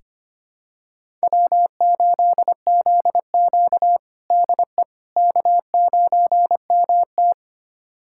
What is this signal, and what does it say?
Morse code, heard as a keyed tone, about 25 wpm, 700 Hz. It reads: W8ZQ DE K9MT
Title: USA Callsign DE Callsign - 4 Characters - 25wpm